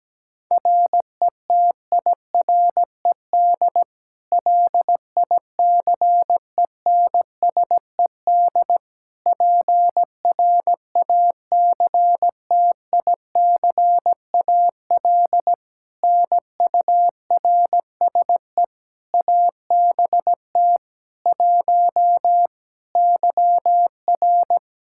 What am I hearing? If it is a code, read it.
Morse code, heard as a keyed tone, about 17 wpm, 700 Hz. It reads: RETIRED LICENSED PRACTICAL NURSE ABT 1 YR
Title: Practice QSO - Exchange 3 - More Info - 17wpm